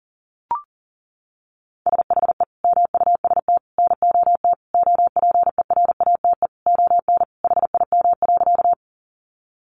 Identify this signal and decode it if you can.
Morse code, heard as a keyed tone, about 37 wpm, 700 Hz. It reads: SHE MUST NOT OPERATE ON HIM.
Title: Sentences from Top 1500 Words 40wpm